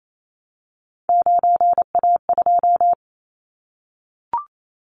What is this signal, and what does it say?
Morse code, heard as a keyed tone, about 28 wpm, 700 Hz. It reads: 9A2
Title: Callsign Prefixes for Top 100 DXCC - 28wpm